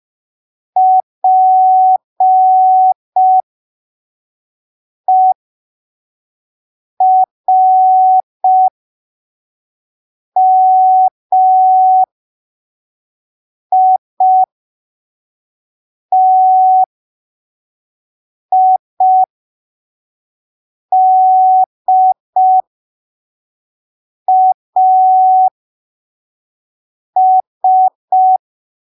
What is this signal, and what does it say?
Morse code, heard as a keyed tone, about 5 wpm, spaced as for 3 wpm, 750 Hz. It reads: PERMITIDAS
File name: arquivos / texto / texto 26 05ppm